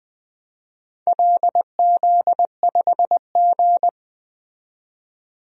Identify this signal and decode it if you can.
Morse code, heard as a keyed tone, about 20 wpm, 700 Hz. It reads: LZ5G
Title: DXCC-100 Call Signs - 4 Characters - 20wpm